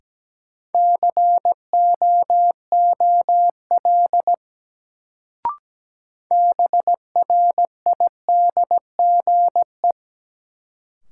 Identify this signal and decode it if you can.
Morse code, heard as a keyed tone, about 17 wpm, 700 Hz. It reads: COOL BRIDGE
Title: Sentences from Top 1200 Words - New Word Drill - 17wpm